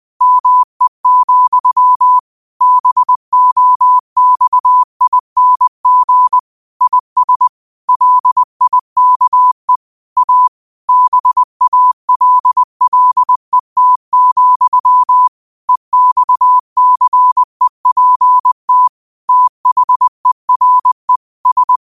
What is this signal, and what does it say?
Morse code, heard as a keyed tone, about 20 wpm, 1000 Hz. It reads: ME, BOXING IS LIKE A BALLET, EXCEPT THERE S
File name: qotdfiles / 20wpm